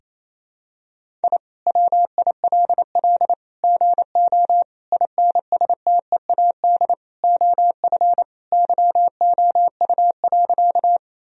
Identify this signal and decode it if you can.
Morse code, heard as a keyed tone, about 28 wpm, 700 Hz. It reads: I WILL GO INSTEAD OF YOU.
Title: Sentences from Top 1000 Words - Review Part 4 - 28wpm